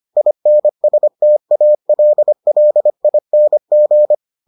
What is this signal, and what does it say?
Morse code, heard as a keyed tone, about 25 wpm, 600 Hz. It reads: INSTALLING